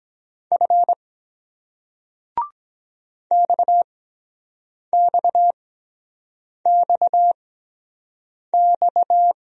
Morse code, heard as a keyed tone, about 23 wpm, spaced as for 7 wpm, 700 Hz. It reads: F XXXX